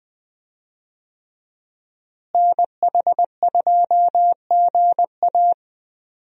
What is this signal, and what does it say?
Morse code, heard as a keyed tone, about 20 wpm, 700 Hz. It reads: NH2GA
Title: USA Call Signs - 5 Characters - Repeated 3x - 20wpm